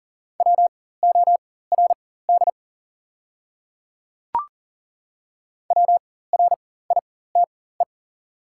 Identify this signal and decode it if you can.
Morse code, heard as a keyed tone, about 42 wpm, spaced as for 19 wpm, 690 Hz. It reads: WORD WRITE
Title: 4x Character Spacing - Top 100 Words - 40wpm